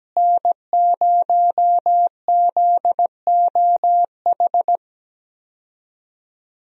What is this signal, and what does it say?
Morse code, heard as a keyed tone, about 17 wpm, 700 Hz. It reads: N0ZOH
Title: USA Call Signs - 5 Characters - Repeated 2x - 17wpm